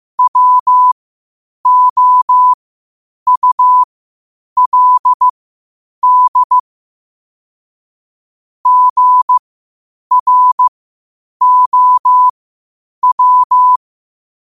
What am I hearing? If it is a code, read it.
Morse code, heard as a keyed tone, about 15 wpm, spaced as for 8 wpm, 1000 Hz. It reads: WOULD GROW
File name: qotdfiles / 8wpm